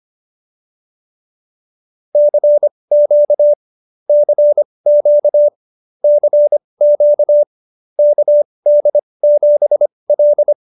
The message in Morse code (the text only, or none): CQ CQ CQ KD7L